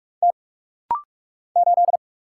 Morse code, heard as a keyed tone, about 44 wpm, 700 Hz. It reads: T 8
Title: Single Letter-Number - Rapid-Fire 45wpm